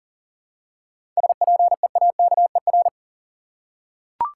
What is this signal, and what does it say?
Morse code, heard as a keyed tone, about 40 wpm, 700 Hz. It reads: SPEAKER
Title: Seven Letter Words - 40wpm